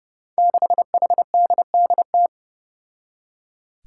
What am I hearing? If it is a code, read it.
Morse code, heard as a keyed tone, about 30 wpm, 700 Hz. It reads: 6HDDT
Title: Five Challenging Letter-Number Combinations 30wpm